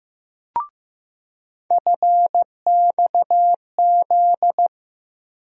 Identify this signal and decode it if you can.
Morse code, heard as a keyed tone, about 15 wpm, 700 Hz. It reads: FXZ
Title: USA Call Sign Postfixes - 3 Letters 15wpm